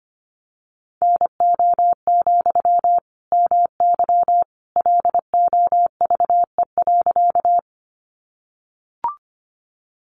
Morse code, heard as a keyed tone, about 25 wpm, 700 Hz. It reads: NO, MY LOVE.
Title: Sentences from Top 400 Words 25wpm